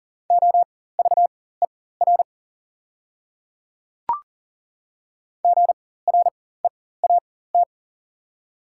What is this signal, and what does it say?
Morse code, heard as a keyed tone, about 37 wpm, spaced as for 18 wpm, 690 Hz. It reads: OVER GREAT
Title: 4x Character Spacing - Top 200 Words - 40wpm